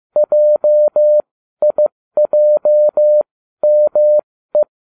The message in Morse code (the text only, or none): JIJME